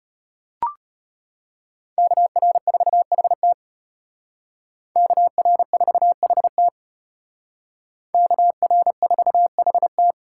Morse code, heard as a keyed tone, about 32 wpm, 700 Hz. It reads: KR4HT KR4HT KR4HT